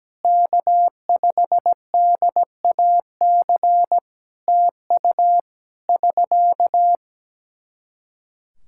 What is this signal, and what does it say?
Morse code, heard as a keyed tone, about 17 wpm, 700 Hz. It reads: K5DAC TU <SK>
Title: Practice QSO - Exchange 4 - Ending - 17wpm